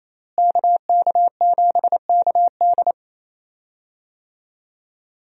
Morse code, heard as a keyed tone, about 28 wpm, 700 Hz. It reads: KK7KD
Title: USA Call Signs - 5 Characters - Repeated 3x - 28wpm